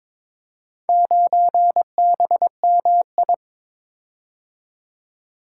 Morse code, heard as a keyed tone, about 22 wpm, 700 Hz. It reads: 9BMI
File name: practice / 22wpm / Four Letter-Number Combinations - 22wpm